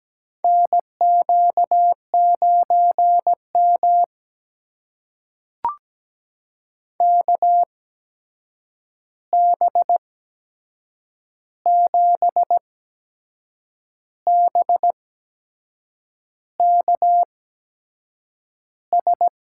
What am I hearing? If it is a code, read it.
Morse code, heard as a keyed tone, about 17 wpm, spaced as for 6 wpm, 700 Hz. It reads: NQ9M KB7BKS
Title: USA Call Signs - 8x Character Spacing - 17wpm